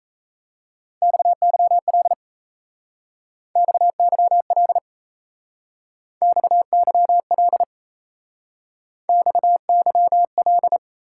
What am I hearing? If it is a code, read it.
Morse code, heard as a keyed tone, about 36 wpm, 700 Hz. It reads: XYL XYL XYL XYL